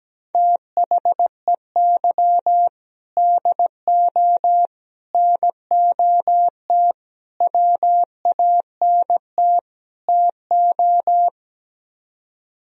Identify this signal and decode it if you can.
Morse code, heard as a keyed tone, about 17 wpm, 700 Hz. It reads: THEY DO NOT WANT TO
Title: Speed-Racing - Sets of 5 Words 17wpm